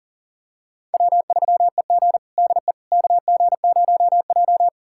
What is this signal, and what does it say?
Morse code, heard as a keyed tone, about 40 wpm, 700 Hz. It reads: W3EG DE KG0J
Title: USA Callsign DE Callsign - 4 Characters - 40wpm